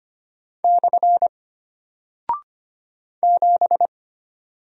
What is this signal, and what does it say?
Morse code, heard as a keyed tone, about 25 wpm, 700 Hz. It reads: / 7
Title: Single Character (Letter:Number:Punctuation:Prosign) Alt 25wpm